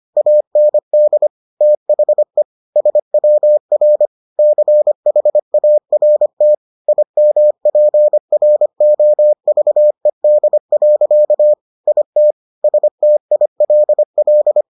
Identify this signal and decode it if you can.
Morse code, heard as a keyed tone, about 25 wpm, 600 Hz. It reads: AND THE SWR CHART IMPROVED. IT STILL